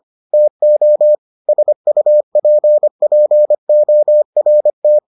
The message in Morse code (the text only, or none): TO SUPPORT